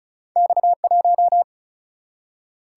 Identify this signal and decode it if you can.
Morse code, heard as a keyed tone, about 36 wpm, 700 Hz. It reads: X1
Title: Single-Letter Single-Number 35wpm